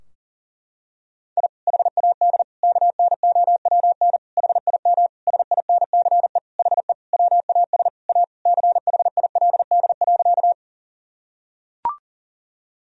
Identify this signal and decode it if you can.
Morse code, heard as a keyed tone, about 40 wpm, 700 Hz. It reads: I HAD KNOWN HIM SINCE HE WAS A CHILD.
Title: Sentences from Top 1000 Words - Review Part 3 - 40wpm